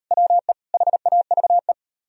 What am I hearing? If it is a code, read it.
Morse code, heard as a keyed tone, about 39 wpm, 700 Hz. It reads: WE HAVE